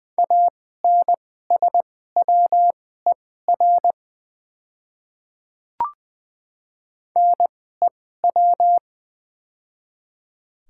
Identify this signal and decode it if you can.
Morse code, heard as a keyed tone, about 20 wpm, spaced as for 14 wpm, 700 Hz. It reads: ANSWER NEW
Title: 2x Character Spacing - Top 300 Words - 20wpm